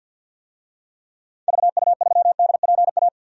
Morse code, heard as a keyed tone, about 47 wpm, 700 Hz. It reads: VU3DPA
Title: DXCC-100 Call Signs - 6 Characters - 50wpm